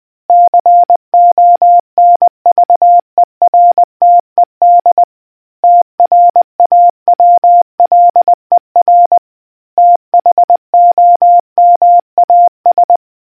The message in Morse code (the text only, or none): CONVERTED TRAWLER THOMAS